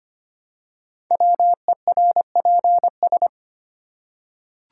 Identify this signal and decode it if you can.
Morse code, heard as a keyed tone, about 25 wpm, 700 Hz. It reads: WERPS
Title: Code Groups - 5 Letters - 25wpm